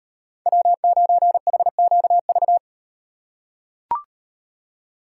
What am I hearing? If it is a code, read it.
Morse code, heard as a keyed tone, about 38 wpm, 700 Hz. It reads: W9HQV